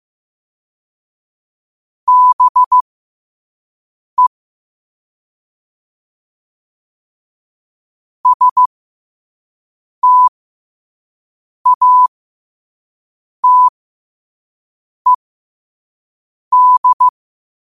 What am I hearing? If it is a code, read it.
Morse code, heard as a keyed tone, about 15 wpm, spaced as for 5 wpm, 1000 Hz. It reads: BE STATED
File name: qotdfiles / 5wpm